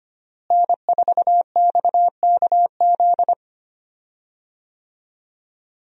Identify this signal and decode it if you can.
Morse code, heard as a keyed tone, about 25 wpm, 700 Hz. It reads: N4XKZ
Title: USA Call Signs - 5 Characters - Repeated 2x - 25wpm